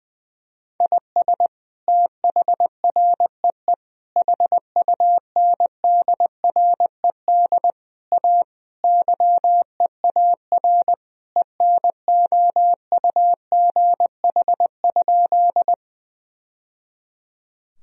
Morse code, heard as a keyed tone, about 21 wpm, 700 Hz. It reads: IS THREE HUNDRED A YEAR ENOUGH?
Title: Sentences from Top 1000 Words - Review Part 2 - 20wpm